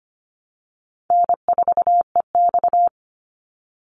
Morse code, heard as a keyed tone, about 25 wpm, 700 Hz. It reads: N4EX
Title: NA SOTA Chaser Callsigns 25wpm